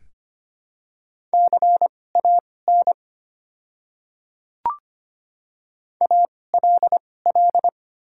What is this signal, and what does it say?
Morse code, heard as a keyed tone, about 25 wpm, spaced as for 18 wpm, 700 Hz. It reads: CAN ALL
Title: 2x Character Spacing - Top 100 Words - 25wpm